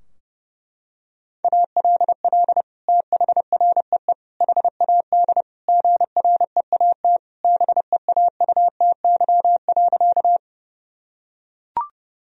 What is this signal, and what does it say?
Morse code, heard as a keyed tone, about 32 wpm, 700 Hz. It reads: ALL THREE HAD GREAT BEAUTY.
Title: Sentences from Top 1000 Words - Review Part 2 - 30wpm